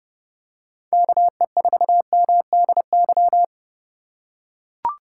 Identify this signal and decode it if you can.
Morse code, heard as a keyed tone, about 30 wpm, 700 Hz. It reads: KE4MDY
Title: USA Call Signs - 3x Character Spacing - 30wpm